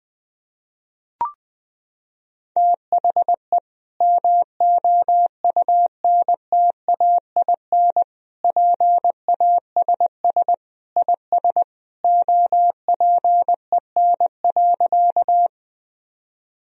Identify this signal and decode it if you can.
Morse code, heard as a keyed tone, about 21 wpm, 700 Hz. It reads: THE MOUNTAIN PASS IS OPEN.
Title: Sentences from Top 1000 Words - Review Part 1 - 20wpm